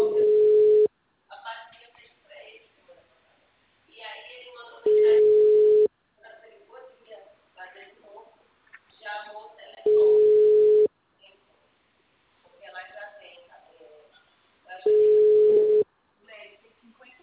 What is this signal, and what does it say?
Ringback tone (Europe, 425 Hz), with speech.